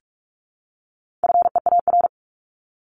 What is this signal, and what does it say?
Morse code, heard as a keyed tone, about 44 wpm, 700 Hz. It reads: FEAR
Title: Top 1751-2000 Words 45wpm